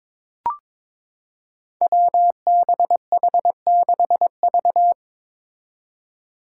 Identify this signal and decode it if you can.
Morse code, heard as a keyed tone, about 22 wpm, 700 Hz. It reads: WBH6V
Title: Five Challenging Letter-Number Combinations 22wpm